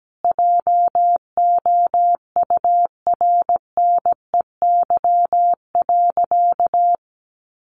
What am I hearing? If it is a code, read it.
Morse code, heard as a keyed tone, about 17 wpm, 700 Hz. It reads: JOURNEY.
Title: A Sound Of Thunder 17wpm